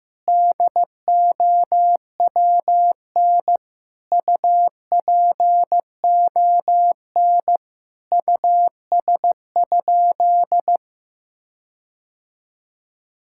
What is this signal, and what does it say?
Morse code, heard as a keyed tone, about 16 wpm, 700 Hz. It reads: DOWN UPON US?
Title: Sentences from Top 1600 Words 15wpm